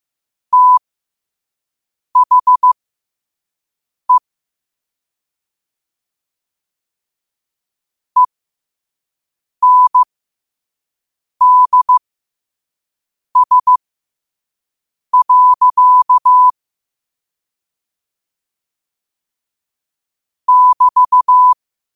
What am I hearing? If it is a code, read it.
Morse code, heard as a keyed tone, about 15 wpm, spaced as for 5 wpm, 1000 Hz. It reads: THE ENDS. <BT>
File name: qotdfiles / 5wpm